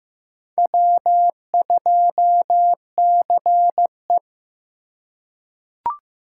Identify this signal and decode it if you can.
Morse code, heard as a keyed tone, about 15 wpm, 700 Hz. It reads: W2CE
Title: USA Call Signs - 2x Character Spacing - 15wpm